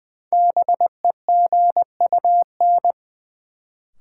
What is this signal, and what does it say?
Morse code, heard as a keyed tone, about 20 wpm, 700 Hz. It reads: BEGUN
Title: Sentences from Top 1000 Words - Review Part 6 - New Word Drill - 20wpm